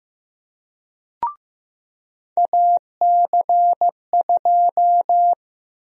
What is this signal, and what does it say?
Morse code, heard as a keyed tone, about 15 wpm, 700 Hz. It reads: AC2
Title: USA Call Sign Prefixes - Repeated 2x - 15wpm